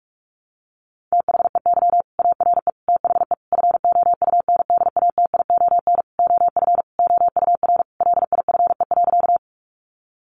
Morse code, heard as a keyed tone, about 44 wpm, 700 Hz. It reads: THEY ARE THE FOUNDATION OF OUR LIFE.
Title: Sentences from Top 1200 Words 45wpm